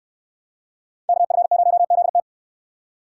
Morse code, heard as a keyed tone, about 69 wpm, 700 Hz. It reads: KF0ZT